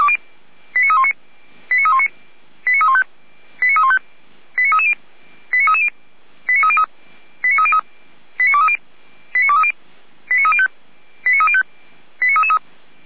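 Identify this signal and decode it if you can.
Five-tone selective call (ZVEI), heard as quick runs of five tones, tones of about 70 ms. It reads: 89319 89319 89315 89315 89330 89330 89303 89303 89230 89230 89305 89305 89333
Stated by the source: ZVEI five-tone selective call received on an SDR.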